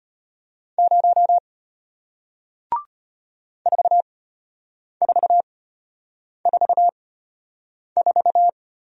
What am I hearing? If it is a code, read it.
Morse code, heard as a keyed tone, about 34 wpm, spaced as for 8 wpm, 700 Hz. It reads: 0 4444